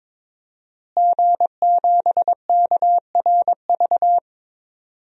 Morse code, heard as a keyed tone, about 22 wpm, 700 Hz. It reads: G7KRV
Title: DXCC-100 Call Signs - 5 Characters - Repeated 2x - 22wpm